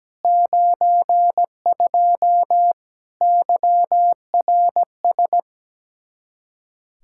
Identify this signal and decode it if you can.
Morse code, heard as a keyed tone, about 17 wpm, 700 Hz. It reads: 92 YRS
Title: Practice QSO - Exchange 3 - More Info - 17wpm